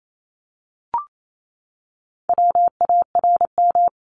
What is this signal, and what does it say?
Morse code, heard as a keyed tone, about 28 wpm, 700 Hz. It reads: WARM
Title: A Sound Of Thunder 28wpm